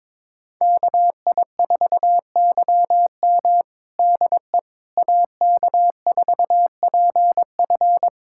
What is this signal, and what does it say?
Morse code, heard as a keyed tone, about 22 wpm, 700 Hz. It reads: KI4YM DE AK4PF
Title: USA Callsign DE Callsign - 5 Characters - 22wpm